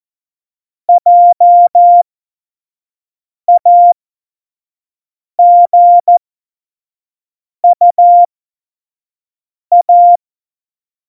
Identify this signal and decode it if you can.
Morse code, heard as a keyed tone, about 14 wpm, spaced as for 5 wpm, 700 Hz. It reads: JAGUA
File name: Morsekurs / Witze / Witz 13. Oktober 2020, 25 F70